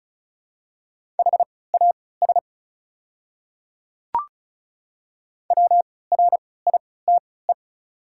Morse code, heard as a keyed tone, about 36 wpm, spaced as for 20 wpm, 690 Hz. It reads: HAS WRITE
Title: 3x Character Spacing - Top 500 Words - 35wpm